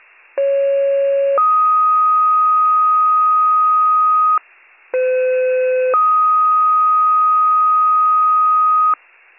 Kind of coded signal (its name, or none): two-tone page